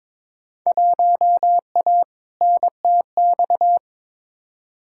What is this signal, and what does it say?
Morse code, heard as a keyed tone, about 22 wpm, 700 Hz. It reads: 1A NTX
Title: Field Day Exchanges - 22wpm